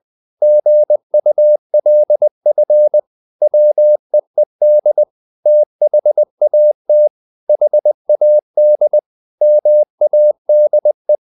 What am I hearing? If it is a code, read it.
Morse code, heard as a keyed tone, about 20 wpm, 600 Hz. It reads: GULF WEED THAT HAD MADE
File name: the old man and the sea / the old man and the sea part 1 20wpm